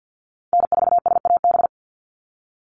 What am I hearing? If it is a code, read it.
Morse code, heard as a keyed tone, about 48 wpm, 700 Hz. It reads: N4SAB
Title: USA Call Signs 50wpm